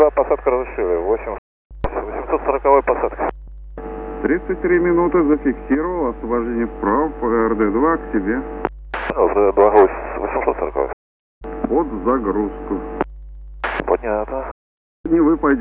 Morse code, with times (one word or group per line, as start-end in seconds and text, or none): none